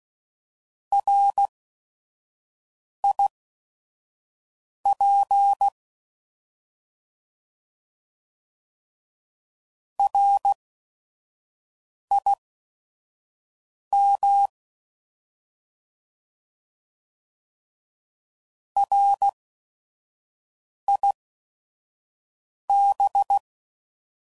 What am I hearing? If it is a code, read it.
Morse code, heard as a keyed tone, about 16 wpm, spaced as for 5 wpm, 800 Hz. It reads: RIP RIM RIB
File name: Prefixes / By First Letter / Rs